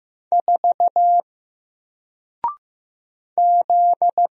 Morse code, heard as a keyed tone, about 15 wpm, 700 Hz. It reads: 4 Z
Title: Single Letter-Number Alt 15wpm